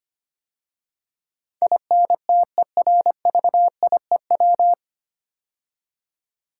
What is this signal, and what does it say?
Morse code, heard as a keyed tone, about 25 wpm, 700 Hz. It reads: INTERVIEW
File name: practice / 25wpm / 9 Letter Words - First 3 Letters Given - 25wpm